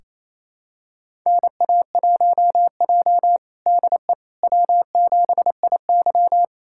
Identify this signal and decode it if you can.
Morse code, heard as a keyed tone, about 28 wpm, 700 Hz. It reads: NA1J DE W7IY
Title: USA Callsign DE Callsign - 4 Characters - 28wpm